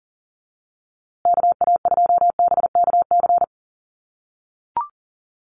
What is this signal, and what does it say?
Morse code, heard as a keyed tone, about 40 wpm, 700 Hz. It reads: KA2BKC
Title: USA Call Signs Repeated 3x 40wpm